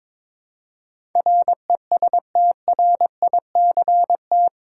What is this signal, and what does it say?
Morse code, heard as a keyed tone, about 21 wpm, 700 Hz. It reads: RESTRICT
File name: practice / 22wpm / Eight Letter Words - 22wpm